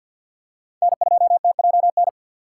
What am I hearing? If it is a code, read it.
Morse code, heard as a keyed tone, about 48 wpm, 690 Hz. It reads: N2TJN
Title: USA Call Signs - 5 Characters - 3x Character Spacing - 50wpm